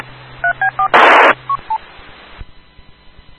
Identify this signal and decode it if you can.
DTMF, heard as telephone touch tones, tones of about 100 ms. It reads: 3B4
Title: Rocky Mountain Radio League Repeaters